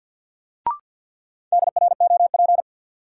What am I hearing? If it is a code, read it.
Morse code, heard as a keyed tone, about 47 wpm, 700 Hz. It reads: DROP
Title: Sentences from Top 1400 Words - New Word Drill - 50wpm